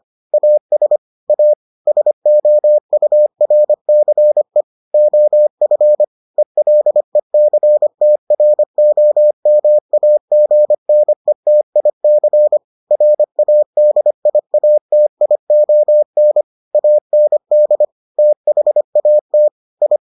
Morse code, heard as a keyed tone, about 25 wpm, 600 Hz. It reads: AS A SOURCE OF ELECTROMAGNETIC RADIATION AND THAT I